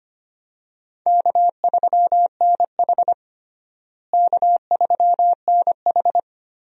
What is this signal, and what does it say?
Morse code, heard as a keyed tone, about 25 wpm, 700 Hz. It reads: K3NH K3NH
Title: USA Call Signs - 4 Characters - Repeated 3x - 25wpm